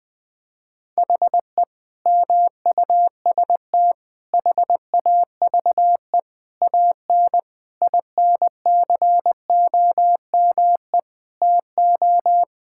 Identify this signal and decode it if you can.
Morse code, heard as a keyed tone, about 21 wpm, 700 Hz. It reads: HE MUST HAVE AN INCOME TO
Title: Sentences from Top 1000 Words - Review Part 5 - 20wpm